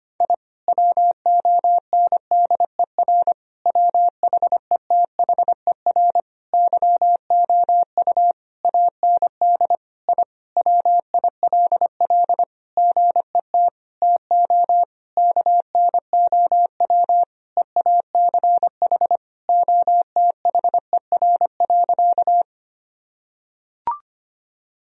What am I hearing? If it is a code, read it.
Morse code, heard as a keyed tone, about 25 wpm, 700 Hz. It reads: I WONDER WHETHER YOU AND I WILL GET TO KNOW EACH OTHER.
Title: Sentences from Top 1000 Words - Review Part 3 - 25wpm